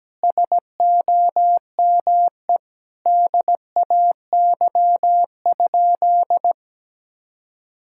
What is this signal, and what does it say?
Morse code, heard as a keyed tone, about 17 wpm, 700 Hz. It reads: SOME DAY?
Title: Sentences from Top 1000 Words - Review Part 6 - 17wpm